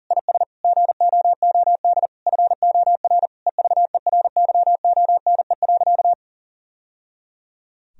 Morse code, heard as a keyed tone, about 40 wpm, 700 Hz. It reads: IS GOOD FOR EVERYONE.
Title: Sentences from Top 1000 Words - Review Part 4 - 40wpm